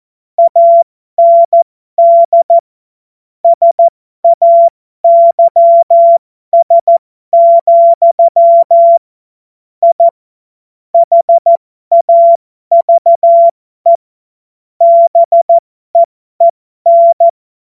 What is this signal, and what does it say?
Morse code, heard as a keyed tone, about 14 wpm, spaced as for 12 wpm, 680 Hz. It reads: AND SAYS, I HAVE BEEN